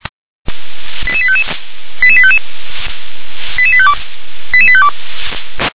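Five-tone selective call (ZVEI), heard as quick runs of five tones, tones of about 70 ms. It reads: 88855 88855 88852 88852
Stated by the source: ZVEI five-tone call received by SDR.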